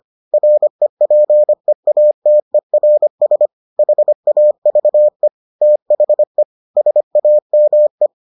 Morse code, heard as a keyed tone, about 25 wpm, 600 Hz. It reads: REPEATERS HAVE THE SAME